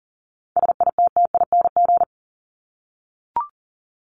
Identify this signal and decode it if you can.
Morse code, heard as a keyed tone, about 40 wpm, 700 Hz. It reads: SITTING
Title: ING Suffix - Top 100 Words with Base Words 40wpm